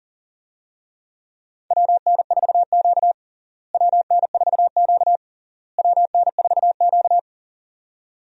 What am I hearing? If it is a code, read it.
Morse code, heard as a keyed tone, about 40 wpm, 700 Hz. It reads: WN4Q WN4Q WN4Q
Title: USA Call Signs - 4 Characters - Repeated 3x - 40wpm